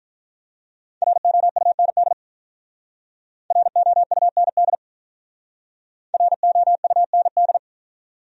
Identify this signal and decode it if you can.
Morse code, heard as a keyed tone, about 44 wpm, 700 Hz. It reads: ROUND ROUND ROUND